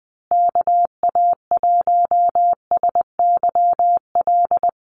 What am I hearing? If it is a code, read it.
Morse code, heard as a keyed tone, about 20 wpm, 700 Hz. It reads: KA1SYL
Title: USA Call Signs Repeated 3x 20wpm